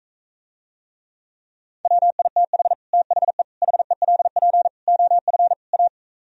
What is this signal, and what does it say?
Morse code, heard as a keyed tone, about 45 wpm, 700 Hz. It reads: WITH THE HELP OF A